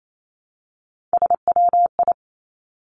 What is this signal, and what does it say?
Morse code, heard as a keyed tone, about 28 wpm, 700 Hz. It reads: SWI
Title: Three Challenging Letter-Number Combinations 28wpm